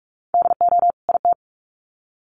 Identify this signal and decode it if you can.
Morse code, heard as a keyed tone, about 44 wpm, 700 Hz. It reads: DO IT
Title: Sets of 2 Words 45wpm